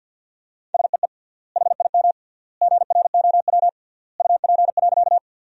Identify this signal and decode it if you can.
Morse code, heard as a keyed tone, about 50 wpm, 700 Hz. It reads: SEE HIM GROW UP.
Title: 3x Word-Spacing - Sentences from Top 300 Words - 50wpm